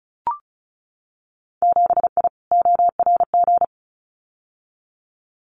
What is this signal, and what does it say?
Morse code, heard as a keyed tone, about 34 wpm, 700 Hz. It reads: 7I ORG
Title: Winter Field Day Exchanges Encore 35wpm